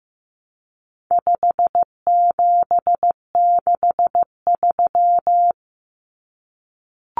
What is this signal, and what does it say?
Morse code, heard as a keyed tone, about 15 wpm, 700 Hz. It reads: 5763
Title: Common Cut Numbers - 4-Digit 15wpm